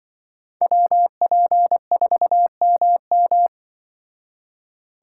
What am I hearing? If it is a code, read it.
Morse code, heard as a keyed tone, about 24 wpm, 700 Hz. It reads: WP4MM
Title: Speed Racing - USA 2x2 Call Signs 20wpm